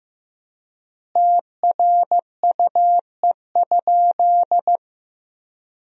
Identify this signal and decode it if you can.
Morse code, heard as a keyed tone, about 15 wpm, 700 Hz. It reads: TRUE?
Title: 3x Word-Spacing - Sentences from Top 400 Words - 15wpm